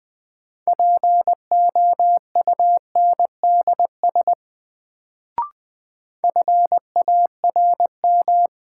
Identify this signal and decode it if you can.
Morse code, heard as a keyed tone, about 20 wpm, 700 Hz. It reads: POUNDS FARM
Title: Sentences from Top 1000 Words - Review Part 2 - New Word Drill - 20wpm